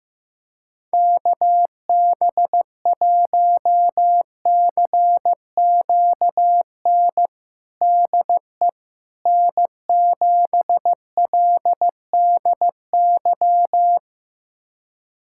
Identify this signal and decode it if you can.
Morse code, heard as a keyed tone, about 15 wpm, 700 Hz. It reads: KB1CQN DE N7LDY
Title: USA Callsign DE Callsign 15wpm